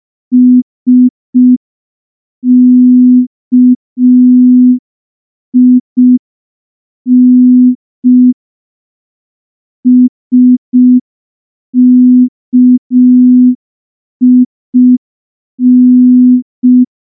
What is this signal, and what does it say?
Morse code, heard as a keyed tone, about 5 wpm, 250 Hz. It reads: SKIN SKIN
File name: morse / morse-training-05WPM